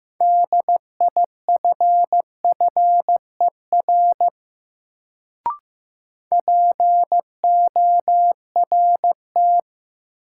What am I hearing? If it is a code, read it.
Morse code, heard as a keyed tone, about 15 wpm, 700 Hz. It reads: DIFFER PORT
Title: Sentences from Top 1000 Words - Review Part 1 - New Word Drill - 15wpm